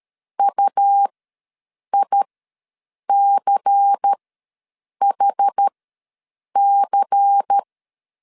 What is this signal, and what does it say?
Morse code, heard as a keyed tone, about 13 wpm, spaced as for 7 wpm, 800 Hz. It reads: UICHC
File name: LEZIONI / 5 LBVC / 5 LBVC ws 12wpm 3159